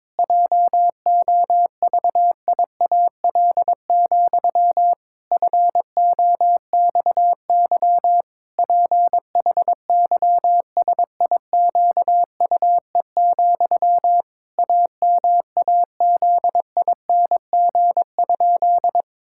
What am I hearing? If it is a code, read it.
Morse code, heard as a keyed tone, about 22 wpm, 700 Hz. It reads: JOVIAL, FOXY PHYSIQUE, AMAZING?
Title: Pangram Challenge v2 - 22wpm